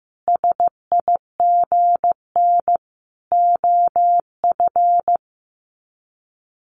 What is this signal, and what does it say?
Morse code, heard as a keyed tone, about 15 wpm, 700 Hz. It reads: SIGN OF
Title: Sets of 5 Words Encore2 15wpm